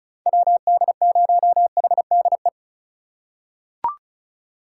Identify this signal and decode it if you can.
Morse code, heard as a keyed tone, about 36 wpm, 700 Hz. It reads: WD0HDE
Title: USA Call Signs - 6 Characters - 4x Character Spacing - 35wpm